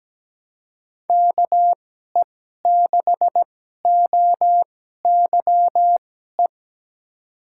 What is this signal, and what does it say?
Morse code, heard as a keyed tone, about 17 wpm, spaced as for 12 wpm, 700 Hz. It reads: KE6OYE
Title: USA Call Signs - 2x Character Spacing - 17wpm